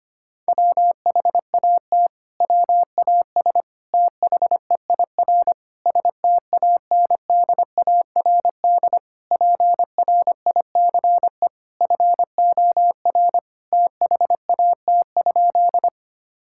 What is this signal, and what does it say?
Morse code, heard as a keyed tone, about 25 wpm, 700 Hz. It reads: WHAT WAS THEIR STANDARD PRICE FOR THAT?
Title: Sentences from Top 1000 Words - Review Part 6 - 25wpm